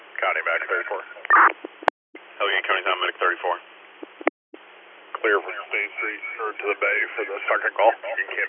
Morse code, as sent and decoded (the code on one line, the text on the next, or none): none